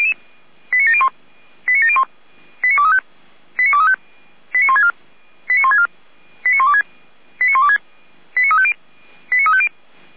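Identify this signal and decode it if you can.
Five-tone selective call (ZVEI), heard as quick runs of five tones, tones of about 70 ms. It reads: 89701 89701 89235 89235 89164 89164 89126 89126 89350 89350
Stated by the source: ZVEI five-tone selective call received on SDR.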